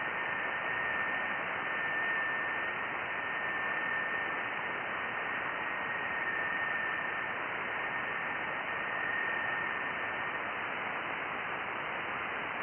FT8.
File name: IC-7100